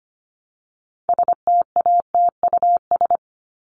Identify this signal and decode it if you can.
Morse code, heard as a keyed tone, about 25 wpm, 700 Hz. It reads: STATUS
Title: Top 1251-1500 Words 25wpm